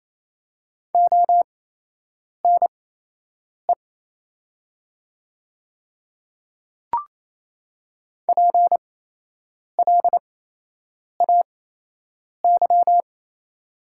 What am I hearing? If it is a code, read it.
Morse code, heard as a keyed tone, about 28 wpm, spaced as for 8 wpm, 700 Hz. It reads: ONE PLAY
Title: 8x Character Spacing - Top 300 Words - 28wpm